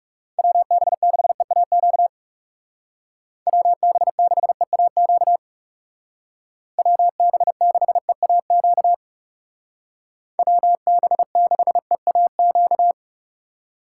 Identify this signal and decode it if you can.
Morse code, heard as a keyed tone, about 40 wpm, 700 Hz. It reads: WB6EAQ WB6EAQ WB6EAQ WB6EAQ